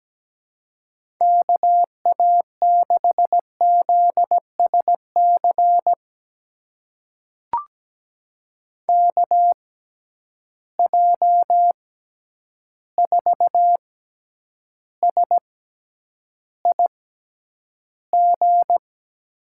Morse code, heard as a keyed tone, about 17 wpm, spaced as for 9 wpm, 700 Hz. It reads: KA6ZSC KJ4SIG
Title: USA Call Signs - 6 Characters - 6x Character Spacing - 17wpm